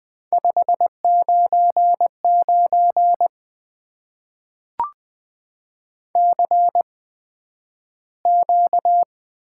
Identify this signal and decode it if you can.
Morse code, heard as a keyed tone, about 20 wpm, spaced as for 11 wpm, 700 Hz. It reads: 599 CQ
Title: Top 90 QSO Elements - 8x Character Spacing - 20wpm